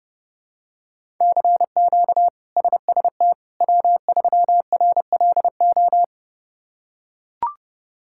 Morse code, heard as a keyed tone, about 32 wpm, 700 Hz. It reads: CQ SST W3RLO
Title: K1USN SST QSO Practice - 30wpm